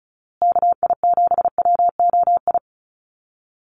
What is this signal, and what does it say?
Morse code, heard as a keyed tone, about 34 wpm, 700 Hz. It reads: KI7WOI
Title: USA Call Signs Repeated 2x 35wpm